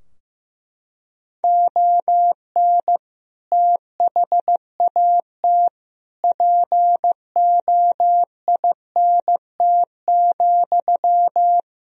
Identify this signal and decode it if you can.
Morse code, heard as a keyed tone, about 15 wpm, 700 Hz. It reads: ON THAT POINT,
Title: Sentences from Top 1000 Words - Review Part 4 - 15wpm